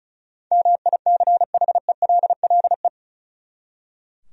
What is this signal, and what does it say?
Morse code, heard as a keyed tone, about 36 wpm, 700 Hz. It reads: MICHELLE
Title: USA Common Names - 35wpm